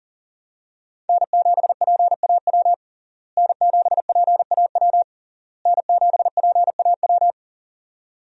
Morse code, heard as a keyed tone, about 37 wpm, 700 Hz. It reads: N7PAW N7PAW N7PAW
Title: USA Call Signs - 5 Characters - Repeated 3x - 40wpm